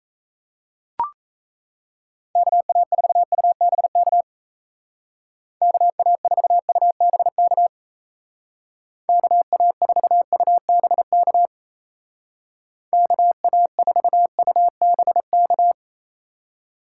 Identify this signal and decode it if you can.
Morse code, heard as a keyed tone, about 34 wpm, 700 Hz. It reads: KA4UBK KA4UBK KA4UBK KA4UBK
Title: Speed Racing - USA 2x3 Call Signs 28wpm